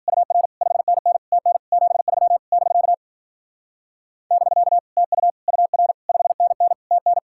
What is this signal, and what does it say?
Morse code, heard as a keyed tone, about 52 wpm, 700 Hz. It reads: UR 5NN TN 73 <BK> <BK> TU UR 5NN TN